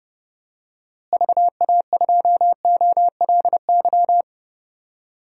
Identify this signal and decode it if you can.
Morse code, heard as a keyed tone, about 32 wpm, 700 Hz. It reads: VA2OLY
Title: DXCC-100 Call Signs - 6 Characters - Repeated 2x - 30wpm